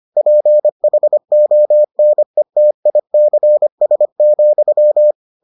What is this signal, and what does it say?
Morse code, heard as a keyed tone, about 25 wpm, 600 Hz. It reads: PHONETICS,